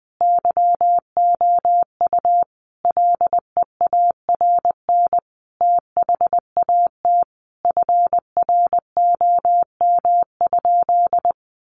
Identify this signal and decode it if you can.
Morse code, heard as a keyed tone, about 19 wpm, 700 Hz. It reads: YOU LEARN THAT FROM?
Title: Sentences from Top 300 Words 20wpm